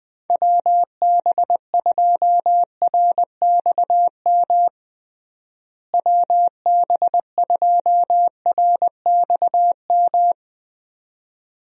Morse code, heard as a keyed tone, about 21 wpm, 700 Hz. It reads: WB2RXM WB2RXM
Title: USA Call Signs - 6 Characters - Repeated 2x - 20wpm